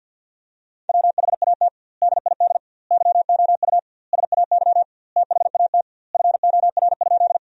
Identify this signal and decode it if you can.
Morse code, heard as a keyed tone, about 53 wpm, 700 Hz. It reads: WHAT DID YOU SAY THAT FOR?
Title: 2x Word-Spacing - Sentences from Top 200 Words - 50wpm